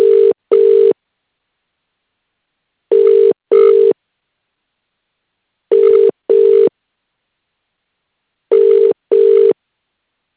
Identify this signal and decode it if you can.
Ringback tone (UK), 400 + 450 Hz.